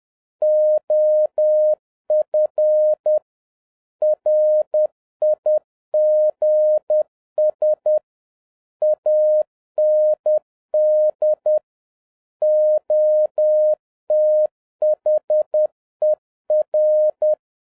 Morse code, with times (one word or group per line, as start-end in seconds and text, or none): 0.4-3.2 OF
4.0-8.0 RIGS
8.8-11.6 AND
12.4-17.3 OTHER